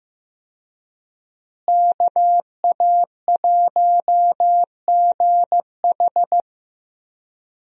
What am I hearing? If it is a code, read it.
Morse code, heard as a keyed tone, about 15 wpm, 700 Hz. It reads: KA1GH
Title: USA Call Signs - 5 Characters - Repeated 3x - 15wpm